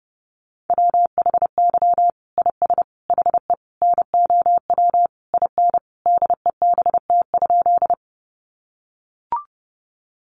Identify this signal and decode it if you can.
Morse code, heard as a keyed tone, about 30 wpm, 700 Hz. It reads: WHY IS HE NOW IN DEBT?
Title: Sentences from Top 1300 Words 30wpm